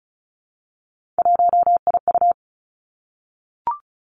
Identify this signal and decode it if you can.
Morse code, heard as a keyed tone, about 36 wpm, 700 Hz. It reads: 1IU
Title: Three Challenging Letter-Number Combinations 35wpm